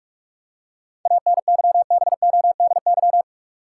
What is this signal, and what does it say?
Morse code, heard as a keyed tone, about 46 wpm, 700 Hz. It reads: ANYBODY
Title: Seven Letter Words - 45wpm